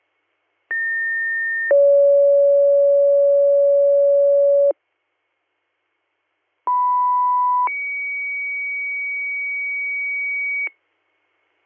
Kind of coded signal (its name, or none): two-tone page